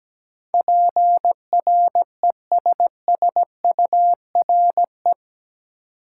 Morse code, heard as a keyed tone, about 17 wpm, 700 Hz. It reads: PRESSURE
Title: Sentences from Top 1000 Words - Review Part 6 - New Word Drill - 17wpm